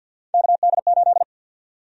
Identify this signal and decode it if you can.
Morse code, heard as a keyed tone, about 53 wpm, 700 Hz. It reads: KD8
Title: USA Call Sign Prefixes - Repeated 2x - 50wpm